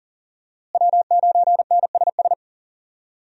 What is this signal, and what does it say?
Morse code, heard as a keyed tone, about 37 wpm, 690 Hz. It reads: W9NSS
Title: USA Call Signs - 5 Characters - Repeated 3x - 40wpm